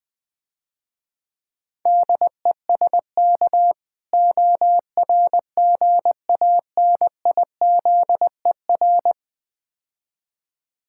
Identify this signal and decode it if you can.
Morse code, heard as a keyed tone, about 20 wpm, 700 Hz. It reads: DESK ORGANIZER
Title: Easy Words - Speed-Racing with Hints - 20wpm